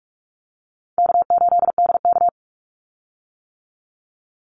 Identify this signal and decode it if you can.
Morse code, heard as a keyed tone, about 46 wpm, 700 Hz. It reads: K8DK
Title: Easy Call Signs - 45wpm